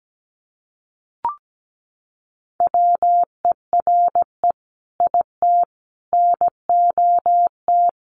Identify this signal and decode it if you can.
Morse code, heard as a keyed tone, about 17 wpm, 700 Hz. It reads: WERE IT NOT
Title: Sentences from Top 900 Words 17wpm